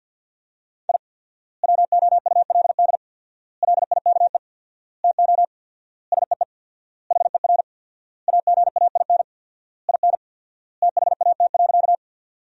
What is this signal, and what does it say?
Morse code, heard as a keyed tone, about 52 wpm, 700 Hz. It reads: I WOULD LIKE TO SEE HER AGAIN IN THAT.
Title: 4x Word-Spacing - Sentences from Top 200 Words - 50wpm